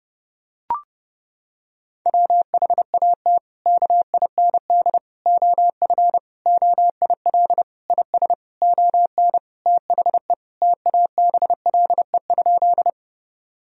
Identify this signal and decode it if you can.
Morse code, heard as a keyed tone, about 30 wpm, 700 Hz. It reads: WHAT KIND OF OIL IS ON THE TABLE?
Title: Sentences from Top 1000 Words - Review Part 4 - 30wpm